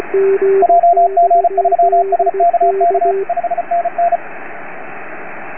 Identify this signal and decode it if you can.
Morse code, heard as a keyed tone, about 37 wpm, 690 Hz, with radio noise. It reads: JG3SKK 5NN